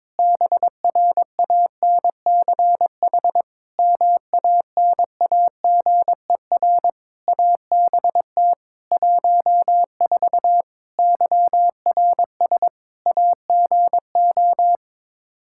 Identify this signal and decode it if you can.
Morse code, heard as a keyed tone, about 21 wpm, 700 Hz. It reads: BRANCH MANAGER ABT 14 YRS AGO
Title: Practice QSO - Exchange 3 - More Info - 22wpm